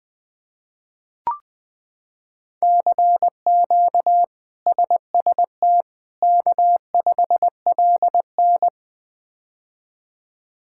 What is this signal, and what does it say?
Morse code, heard as a keyed tone, about 20 wpm, 700 Hz. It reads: CQ SST K5LN
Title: K1USN SST QSO Practice - 20wpm